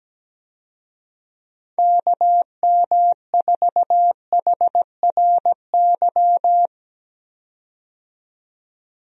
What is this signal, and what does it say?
Morse code, heard as a keyed tone, about 17 wpm, 700 Hz. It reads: KM4HRY
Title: USA Call Signs - 6 Characters - 17wpm